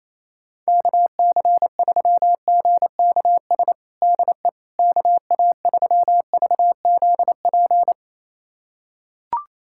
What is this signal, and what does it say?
Morse code, heard as a keyed tone, about 28 wpm, 700 Hz. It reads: KC3GKS DE KA3VZP
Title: USA Callsign DE Callsign - 6 Characters - 28wpm